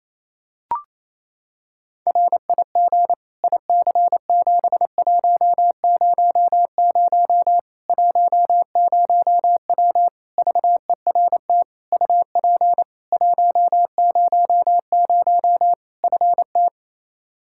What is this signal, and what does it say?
Morse code, heard as a keyed tone, about 28 wpm, 700 Hz. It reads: RIG IC7100 10W VERT UP 100 FT
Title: Practice QSO - Exchange 2 - Info - 28wpm